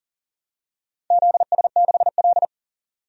Morse code, heard as a keyed tone, about 40 wpm, 700 Hz. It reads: ZS6L
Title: DXCC-100 Call Signs - 4 Characters - 40wpm